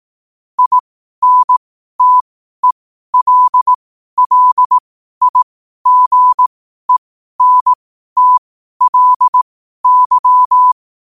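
Morse code, heard as a keyed tone, about 18 wpm, spaced as for 12 wpm, 1000 Hz. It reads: INTELLIGENTLY
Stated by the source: Morse code at 12 wpm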